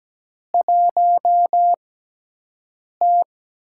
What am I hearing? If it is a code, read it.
Morse code, heard as a keyed tone, about 17 wpm, spaced as for 6 wpm, 700 Hz. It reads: 1T
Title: USA Call Signs - 4 Characters - 6x Character Spacing - 17wpm